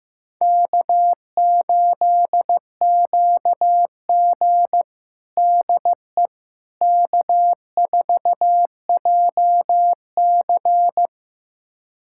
Morse code, heard as a keyed tone, about 15 wpm, 700 Hz. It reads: K8QG DE K4JC
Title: USA Callsign DE Callsign - 4 Characters - 15wpm